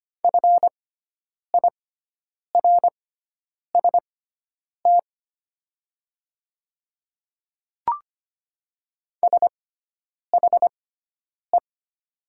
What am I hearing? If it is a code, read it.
Morse code, heard as a keyed tone, about 25 wpm, spaced as for 9 wpm, 700 Hz. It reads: FIRST SHE
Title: 6x Character Spacing - Top 200 Words - 25wpm